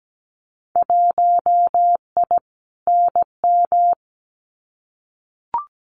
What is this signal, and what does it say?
Morse code, heard as a keyed tone, about 17 wpm, 700 Hz. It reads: 1I NM
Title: Winter Field Day Exchanges Encore 17wpm